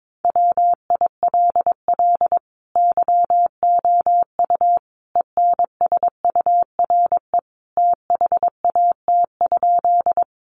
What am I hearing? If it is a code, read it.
Morse code, heard as a keyed tone, about 22 wpm, 700 Hz. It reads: WILL YOU ENSURE THAT?
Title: Sentences from Top 1100 Words 22wpm